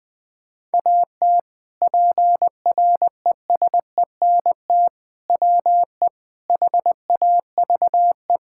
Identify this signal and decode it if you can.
Morse code, heard as a keyed tone, about 20 wpm, 700 Hz. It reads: AT PRESENT WE HAVE
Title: Sentences from Top 1000 Words - Review Part 5 - 20wpm